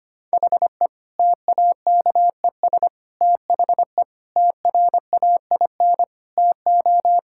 Morse code, heard as a keyed tone, about 25 wpm, 700 Hz. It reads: HE TAKES THE TRAIN TO
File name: practice / 25wpm / Difficult Sentences with Hints - 25wpm